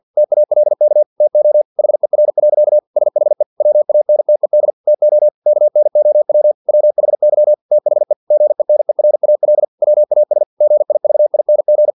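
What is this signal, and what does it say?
Morse code, heard as a keyed tone, about 50 wpm, 600 Hz. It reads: TALK TO HER. SHE WANTED TO KNOW WHY THE GENERAL WAS GIVING
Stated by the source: Morse practice text from an English-language novel at 49 wpm.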